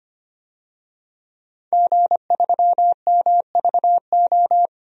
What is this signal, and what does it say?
Morse code, heard as a keyed tone, about 25 wpm, 700 Hz. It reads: G3MVO
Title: DXCC-100 Call Signs - 5 Characters - Repeated 2x - 25wpm